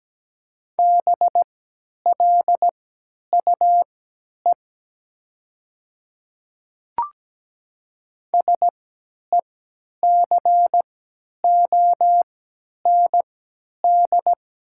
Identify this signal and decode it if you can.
Morse code, heard as a keyed tone, about 17 wpm, spaced as for 10 wpm, 700 Hz. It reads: BLUE SECOND
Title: 3x Character Spacing - Top 500 Words - 17wpm